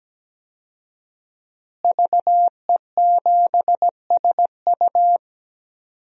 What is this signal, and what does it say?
Morse code, heard as a keyed tone, about 17 wpm, 700 Hz. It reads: VE7SU
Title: DXCC-100 Call Signs - 5 Characters - Repeated 2x - 17wpm